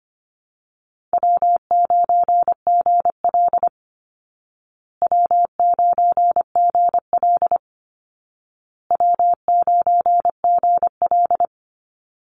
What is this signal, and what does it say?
Morse code, heard as a keyed tone, about 25 wpm, 700 Hz. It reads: W9GL W9GL W9GL
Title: Call Signs Repeated 2x - 25wpm